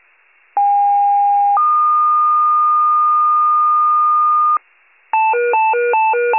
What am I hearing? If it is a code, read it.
Two-tone page, heard as a one-second tone, then a three-second tone, about 810 then 1220 Hz. It is heard from 0.6 to 4.6 s.
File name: pager / 73 RESCUE / wcalert